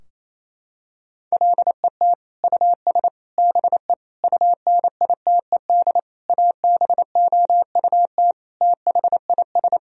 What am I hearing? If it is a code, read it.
Morse code, heard as a keyed tone, about 28 wpm, 700 Hz. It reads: LET US BE UNITED ABOUT THIS
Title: Sentences from Top 1000 Words - Review Part 4 - 28wpm